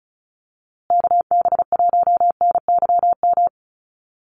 Morse code, heard as a keyed tone, about 34 wpm, 700 Hz. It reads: KB1NYM
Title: USA Call Signs Repeated 2x 35wpm